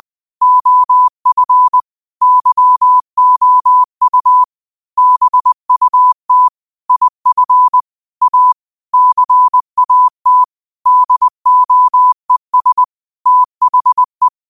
Morse code, heard as a keyed tone, about 20 wpm, 1000 Hz. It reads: OF YOU BUT IF A CAT DOES THE